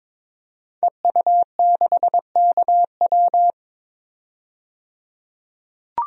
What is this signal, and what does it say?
Morse code, heard as a keyed tone, about 21 wpm, 700 Hz. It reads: EU6KW
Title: DXCC-100 Call Signs - 5 Characters - 22wpm